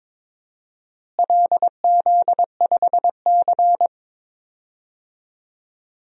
Morse code, heard as a keyed tone, about 22 wpm, 700 Hz. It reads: LZ5C
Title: DXCC-100 Call Signs - 4 Characters - Repeated 3x - 22wpm